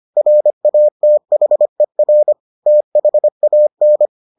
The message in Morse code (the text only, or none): RATHER THAN